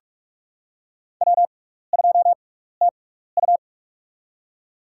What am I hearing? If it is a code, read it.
Morse code, heard as a keyed tone, about 42 wpm, spaced as for 15 wpm, 700 Hz. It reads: W2TU
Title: USA Call Signs - 4 Characters - 6x Character Spacing - 45wpm